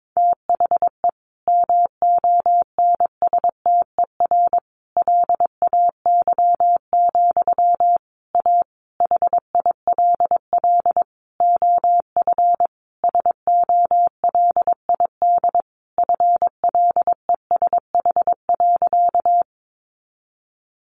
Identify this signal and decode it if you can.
Morse code, heard as a keyed tone, about 22 wpm, 700 Hz. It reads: THE MONSTER LAY, A HILL OF SOLID FLESH.
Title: A Sound Of Thunder 22wpm